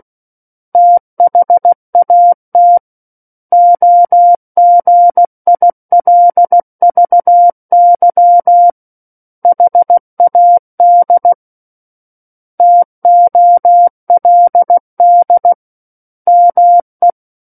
Morse code, heard as a keyed tone, about 16 wpm, 700 Hz. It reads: THAT OGILVY HAD TOLD ME
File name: Book One / 07-Chapter 7 - 16 WPM